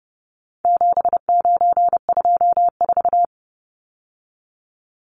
Morse code, heard as a keyed tone, about 30 wpm, 700 Hz. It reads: 7924
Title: Numbers - 4-Digit 30wpm